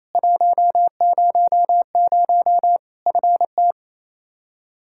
Morse code, heard as a keyed tone, about 28 wpm, 700 Hz. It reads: 100 FT
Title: Practice QSO - Exchange 2 - Info - 28wpm